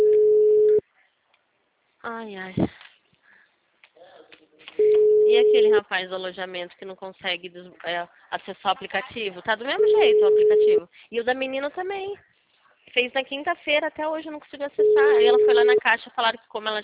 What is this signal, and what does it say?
Ringback tone (Europe, 425 Hz), with speech.
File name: out- -2006- - - .76698